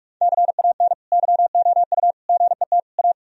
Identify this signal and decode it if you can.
Morse code, heard as a keyed tone, about 46 wpm, 700 Hz. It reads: CAN YOU GET A
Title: Sentences from Top 1000 Words - Review Part 1 - 45wpm